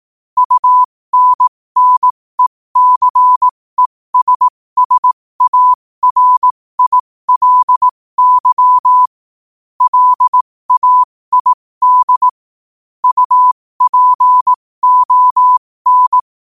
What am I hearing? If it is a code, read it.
Morse code, heard as a keyed tone, about 18 wpm, spaced as for 15 wpm, 1000 Hz. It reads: UNNECESSARILY LAID UPON
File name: qotdfiles / 15wpm